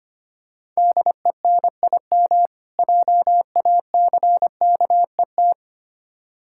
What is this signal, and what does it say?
Morse code, heard as a keyed tone, about 25 wpm, 700 Hz. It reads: DENIM JACKET
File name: practice / 25wpm / Difficult Words with Hints - 25wpm